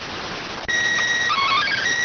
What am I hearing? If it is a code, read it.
SSTV (slow-scan television), PD 290.